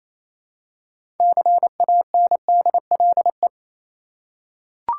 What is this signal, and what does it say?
Morse code, heard as a keyed tone, about 28 wpm, 700 Hz. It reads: CANDLE
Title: Six Letter Words - 28wpm